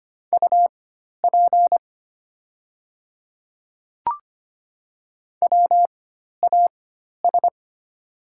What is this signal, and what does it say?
Morse code, heard as a keyed tone, about 25 wpm, spaced as for 12 wpm, 700 Hz. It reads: UP WAS
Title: 4x Character Spacing - Top 200 Words - 25wpm